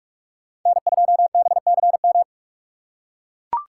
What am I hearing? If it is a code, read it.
Morse code, heard as a keyed tone, about 44 wpm, 700 Hz. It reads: N2BCM